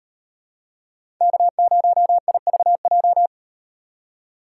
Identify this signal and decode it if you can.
Morse code, heard as a keyed tone, about 38 wpm, 700 Hz. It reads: K0IVJ